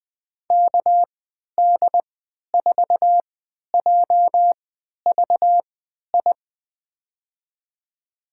Morse code, heard as a keyed tone, about 20 wpm, spaced as for 11 wpm, 700 Hz. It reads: KD4JVI
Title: USA Call Signs - 3x Character Spacing - 20wpm